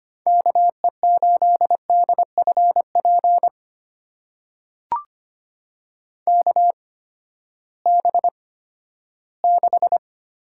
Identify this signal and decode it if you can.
Morse code, heard as a keyed tone, about 25 wpm, spaced as for 14 wpm, 700 Hz. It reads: KE8DFP KB6
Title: USA Call Signs - 6 Characters - 8x Character Spacing - 25wpm